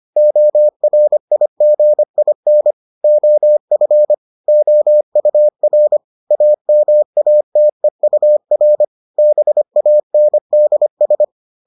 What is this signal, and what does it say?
Morse code, heard as a keyed tone, about 25 wpm, 600 Hz. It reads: ORIGIN OF OUR AMATEUR BANDS